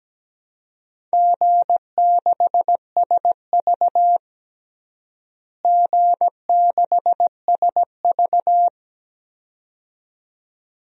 Morse code, heard as a keyed tone, about 17 wpm, 700 Hz. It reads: G6SV G6SV
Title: DXCC-100 Call Signs - 4 Characters - Repeated 2x - 17wpm